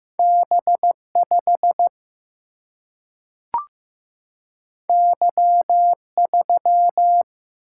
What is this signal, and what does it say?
Morse code, heard as a keyed tone, about 15 wpm, 700 Hz. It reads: B5 Y3
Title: Single-Letter Single-Number 15wpm